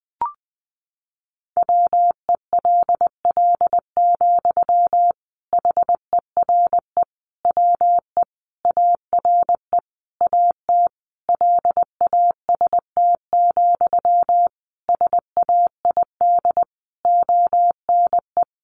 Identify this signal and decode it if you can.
Morse code, heard as a keyed tone, about 20 wpm, 700 Hz. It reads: WELL, HERE WE ARE AT LAST, SAID ONE
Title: Sentences from Top 300 Words 20wpm